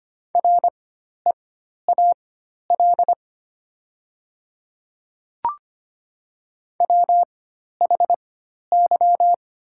Morse code, heard as a keyed tone, about 25 wpm, spaced as for 12 wpm, 690 Hz. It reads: REAL WHY
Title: 4x Character Spacing - Top 300 Words - 25wpm